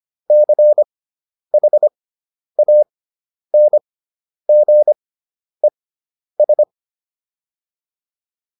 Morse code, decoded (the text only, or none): CHANGES